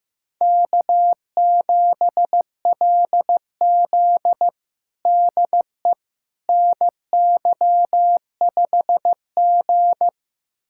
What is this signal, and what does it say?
Morse code, heard as a keyed tone, about 15 wpm, 700 Hz. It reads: K7LZ DE NY5G
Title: USA Callsign DE Callsign - 4 Characters - 15wpm